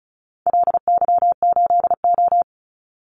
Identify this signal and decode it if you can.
Morse code, heard as a keyed tone, about 36 wpm, 700 Hz. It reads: LY8O
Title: Call Signs - 35wpm